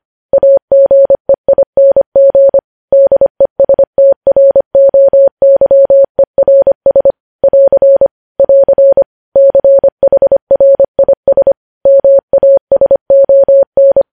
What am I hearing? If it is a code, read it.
Morse code, heard as a keyed tone, about 25 wpm, 550 Hz. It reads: AGEING DESTROYERS <AR> <AR> CHRIS MASON
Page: news-headlines-in-morse-code-at-25-wpm